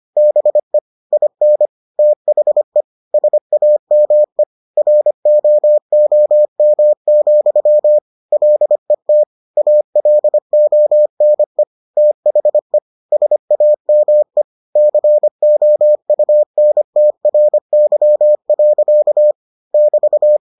Morse code, heard as a keyed tone, about 25 wpm, 600 Hz. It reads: BE IN THE SAME ROOM, LET ALONE THE SAME COUNTRY. <BT>